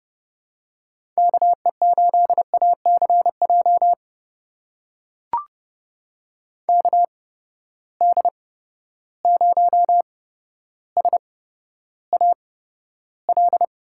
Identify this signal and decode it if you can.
Morse code, heard as a keyed tone, about 30 wpm, spaced as for 13 wpm, 700 Hz. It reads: KE8ACJ KD0SAL
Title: USA Call Signs - 6 Characters - 8x Character Spacing - 30wpm